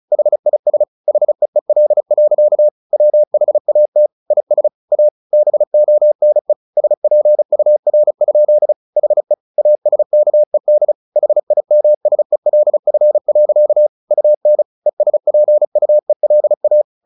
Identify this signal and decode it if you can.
Morse code, heard as a keyed tone, about 34 wpm, 600 Hz. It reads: HIS HEEL. WHAT IS A BONE SPUR? HE ASKED HIMSELF. UN ESPUELA
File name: the old man and the sea / the old man and the sea 1 35wpm